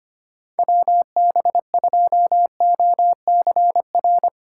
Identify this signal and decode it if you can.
Morse code, heard as a keyed tone, about 32 wpm, 700 Hz. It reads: WB2OCR